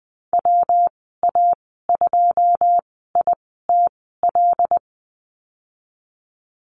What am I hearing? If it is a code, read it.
Morse code, heard as a keyed tone, about 20 wpm, spaced as for 15 wpm, 700 Hz. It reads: WA2ITL
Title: 122 Lesson 20g - Number 3 - Call Signs 20wpm